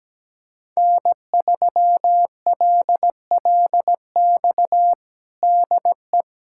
Morse code, heard as a keyed tone, about 17 wpm, 700 Hz. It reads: N3LLX DE
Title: USA Callsign DE Callsign 17wpm